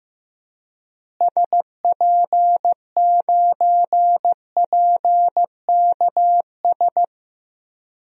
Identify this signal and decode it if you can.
Morse code, heard as a keyed tone, about 15 wpm, 700 Hz. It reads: SP9PKS
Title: DXCC-100 Call Signs - 6 Characters - Repeated 2x - 15wpm